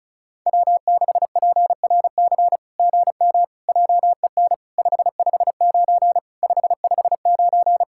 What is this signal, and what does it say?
Morse code, heard as a keyed tone, about 33 wpm, 700 Hz. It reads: W6PRC GM JEN 559 559
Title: SOTA QSO Practice 35wpm